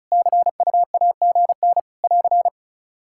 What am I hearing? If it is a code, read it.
Morse code, heard as a keyed tone, about 36 wpm, 700 Hz. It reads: CUAGN <AR>
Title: Practice QSO - Exchange 4 - Ending - 35wpm